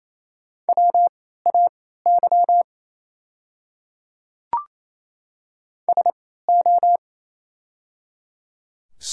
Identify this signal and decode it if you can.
Morse code, heard as a keyed tone, about 27 wpm, spaced as for 16 wpm, 700 Hz. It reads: WAY SO
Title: 3x Character Spacing - Top 200 Words - 28wpm